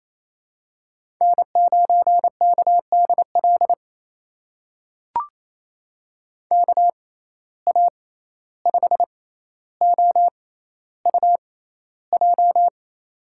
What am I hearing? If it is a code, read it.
Morse code, heard as a keyed tone, about 28 wpm, spaced as for 14 wpm, 700 Hz. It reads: N9KDL KA5OUJ
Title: USA Call Signs - 6x Character Spacing - 28wpm